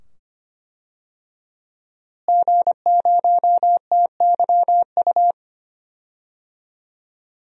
Morse code, heard as a keyed tone, about 25 wpm, 700 Hz. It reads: G0TYU
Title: DXCC-100 Call Signs - 5 Characters - Repeated 2x - 25wpm